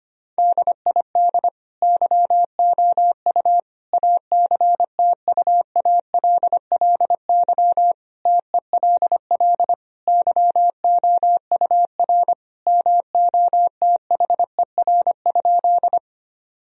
Morse code, heard as a keyed tone, about 25 wpm, 700 Hz. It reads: DID YOU ACTUALLY TELL YOUR MOTHER?
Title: Sentences from Top 1000 Words - Review Part 4 - 25wpm